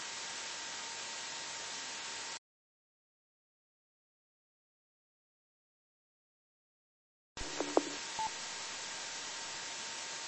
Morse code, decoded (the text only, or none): none